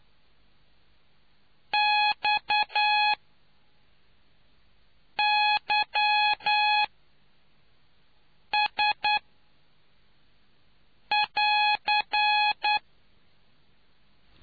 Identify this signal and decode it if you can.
Morse code, heard as a keyed tone, about 9 wpm, spaced as for 4 wpm, 840 Hz. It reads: XYS <AR>